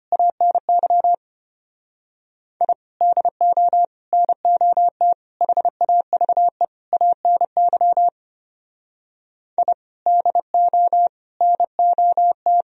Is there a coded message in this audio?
Yes: Morse code from 0.1 to 12.6 s, heard as a keyed tone, about 33 wpm, 700 Hz. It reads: ANY I DO NOT HAVE ANY I DO NOT